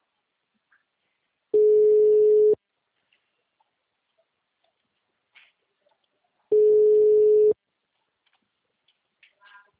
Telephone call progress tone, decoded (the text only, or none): ringback tone